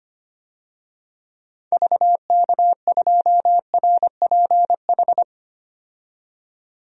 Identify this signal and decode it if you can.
Morse code, heard as a keyed tone, about 25 wpm, 700 Hz. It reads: VK2RPH
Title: DXCC-100 Call Signs - 6 Characters - 25wpm